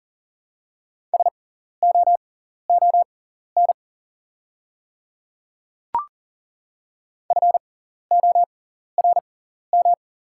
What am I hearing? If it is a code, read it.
Morse code, heard as a keyed tone, about 40 wpm, spaced as for 14 wpm, 690 Hz. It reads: SOON FORM
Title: 6x Character Spacing - Top 500 Words - 40wpm